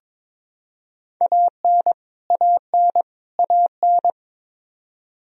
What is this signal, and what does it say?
Morse code, heard as a keyed tone, about 22 wpm, 700 Hz. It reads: AN AN AN
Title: Top 100 Words Repeated 22wpm